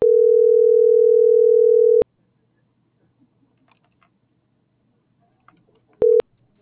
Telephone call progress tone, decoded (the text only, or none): ringback tone